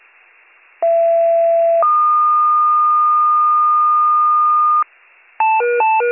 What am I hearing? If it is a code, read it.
Two-tone page, heard as a one-second tone, then a three-second tone, about 670 then 1220 Hz. It is heard from 0.8 to 4.8 s.